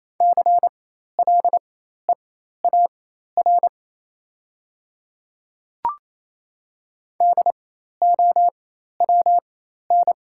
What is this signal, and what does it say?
Morse code, heard as a keyed tone, about 28 wpm, spaced as for 13 wpm, 690 Hz. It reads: CLEAR DOWN
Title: 4x Character Spacing - Top 500 Words - 28wpm